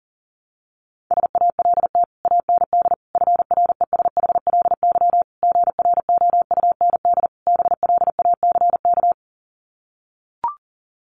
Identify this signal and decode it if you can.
Morse code, heard as a keyed tone, about 40 wpm, 700 Hz. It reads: SALT AND FRESHLY GROUND BLACK
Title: Sets of 5 Words Encore 40wpm